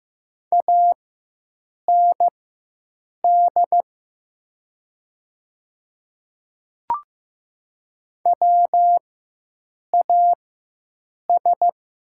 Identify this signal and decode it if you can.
Morse code, heard as a keyed tone, about 15 wpm, spaced as for 7 wpm, 700 Hz. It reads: AND WAS
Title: 4x Character Spacing - Top 100 Words - 15wpm